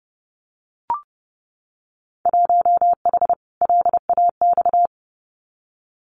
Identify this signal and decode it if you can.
Morse code, heard as a keyed tone, about 30 wpm, 700 Hz. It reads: 1H LAX
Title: Winter Field Day Exchanges 30wpm